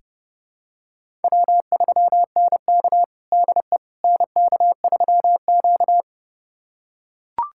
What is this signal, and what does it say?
Morse code, heard as a keyed tone, about 30 wpm, 700 Hz. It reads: W3NK DE NK3Q
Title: USA Callsign DE Callsign - 4 Characters - 30wpm